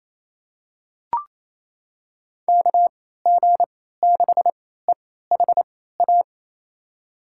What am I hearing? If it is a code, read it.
Morse code, heard as a keyed tone, about 27 wpm, spaced as for 16 wpm, 700 Hz. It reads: KG6EHA
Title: USA Call Signs - 6 Characters - 3x Character Spacing - 28wpm